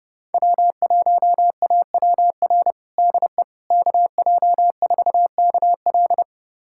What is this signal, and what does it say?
Morse code, heard as a keyed tone, about 30 wpm, 700 Hz. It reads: W1AWR DE KJ4KL
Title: USA Callsign DE Callsign - 5 Characters - 30wpm